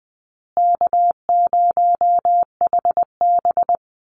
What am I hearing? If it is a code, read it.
Morse code, heard as a keyed tone, about 20 wpm, 700 Hz. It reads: K0HB
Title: Easy Call Signs Repeated 3x - 20wpm